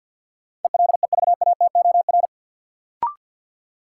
Morse code, heard as a keyed tone, about 53 wpm, 700 Hz. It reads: ELEVATOR
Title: Sentences from Top 1600 Words - New Word Drill - 50wpm